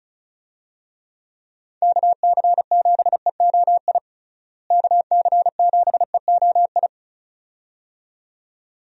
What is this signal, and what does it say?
Morse code, heard as a keyed tone, about 34 wpm, 700 Hz. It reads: KC7EOI KC7EOI
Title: USA Call Signs - 6 Characters - Repeated 2x - 35wpm